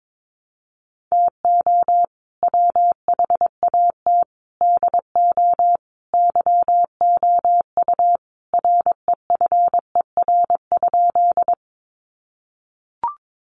Morse code, heard as a keyed tone, about 22 wpm, 700 Hz. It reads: TO WHAT DO YOU REFER?
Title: Sentences from Top 1000 Words 22wpm